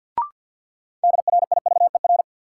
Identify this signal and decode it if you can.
Morse code, heard as a keyed tone, about 46 wpm, 700 Hz. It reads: DRIVER
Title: Sentences from Top 1000 Words - New Word Drill - 50wpm